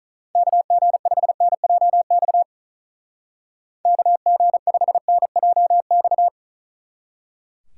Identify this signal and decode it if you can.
Morse code, heard as a keyed tone, about 44 wpm, 700 Hz. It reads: KG5NJX KG5NJX